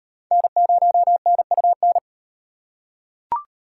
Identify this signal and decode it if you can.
Morse code, heard as a keyed tone, about 38 wpm, 700 Hz. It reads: N0NUN